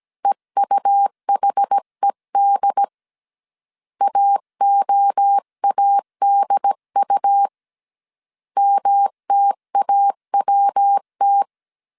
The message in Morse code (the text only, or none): EUHED AOADU MTAWT